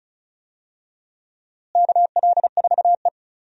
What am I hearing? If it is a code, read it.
Morse code, heard as a keyed tone, about 34 wpm, 700 Hz. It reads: KL4E
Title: USA Call Signs - 4 Characters - Repeated 2x - 35wpm